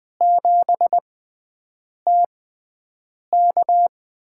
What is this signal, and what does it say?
Morse code, heard as a keyed tone, about 20 wpm, spaced as for 7 wpm, 700 Hz. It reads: 7TK
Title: USA Call Signs - 5 Characters - 6x Character Spacing - 20wpm